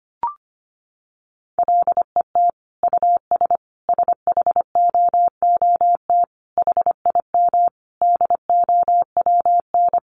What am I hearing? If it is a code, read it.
Morse code, heard as a keyed tone, about 25 wpm, 700 Hz. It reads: LET US SHOOT HIM DOWN
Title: Sentences from Top 1200 Words 25wpm